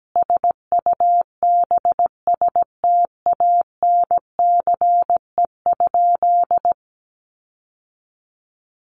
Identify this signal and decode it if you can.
Morse code, heard as a keyed tone, about 17 wpm, 700 Hz. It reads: SUBSTANCE?
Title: Sentences from Top 1500 Words 17wpm